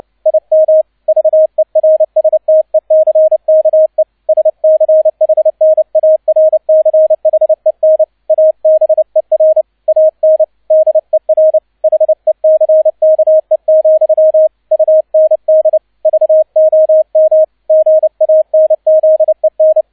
Morse code, heard as a keyed tone, about 29 wpm, 630 Hz, with radio noise. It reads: IM VERSTECKE SCHNARCHEN ABER AN DER HECKE, UND VOM GANZEN
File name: 002 Zweiter Streich 30wpm